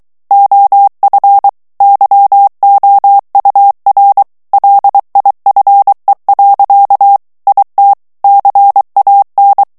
Morse code, heard as a keyed tone, about 23 wpm, 800 Hz. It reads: OF YOUR LIFE. IT CAN
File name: morse mp3 / ANGLAIS texte clair 1 vitesse 22 mots minute